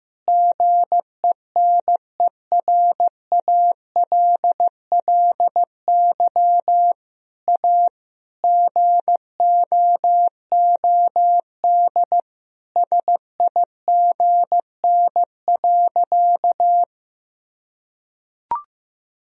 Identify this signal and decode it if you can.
Morse code, heard as a keyed tone, about 15 wpm, 700 Hz. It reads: GENERALLY A GOOD SIGN.
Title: Sentences from Top 1000 Words - Review Part 4 - 15wpm